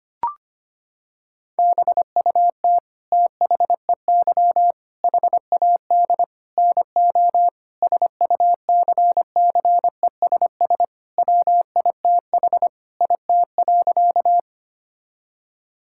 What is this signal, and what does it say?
Morse code, heard as a keyed tone, about 25 wpm, 700 Hz. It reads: BUT THEY HAD NO SUCCESS WITH IT.
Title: Sentences from Top 1000 Words - Review Part 6 - 25wpm